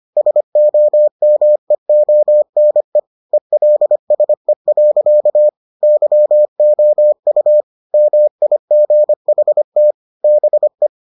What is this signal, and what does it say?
Morse code, heard as a keyed tone, about 25 wpm, 600 Hz. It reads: SOMEONE ELSE. YOU MIGHT BE